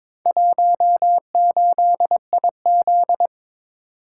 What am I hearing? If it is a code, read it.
Morse code, heard as a keyed tone, about 23 wpm, 700 Hz. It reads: 18IZ
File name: practice / 22wpm / Four Letter-Number Combinations - 22wpm